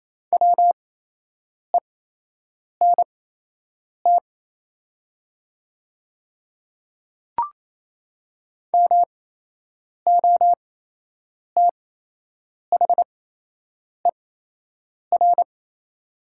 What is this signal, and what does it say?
Morse code, heard as a keyed tone, about 28 wpm, spaced as for 8 wpm, 700 Hz. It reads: WENT MOTHER
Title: 8x Character Spacing - Top 200 Words - 28wpm